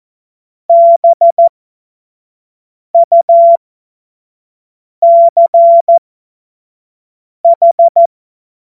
Morse code, heard as a keyed tone, about 14 wpm, spaced as for 5 wpm, 680 Hz. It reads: BUCH
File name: Morsekurs / Witze / Witz 20. Dezember 2022 25 F70